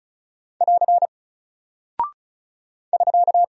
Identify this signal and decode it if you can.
Morse code, heard as a keyed tone, about 36 wpm, 700 Hz. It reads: <AR> <SK>
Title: Single Character (Letter:Number:Punctuation:Prosign) Alt 35wpm